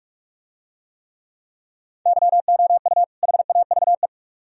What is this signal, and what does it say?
Morse code, heard as a keyed tone, about 44 wpm, 700 Hz. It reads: YOU HAVE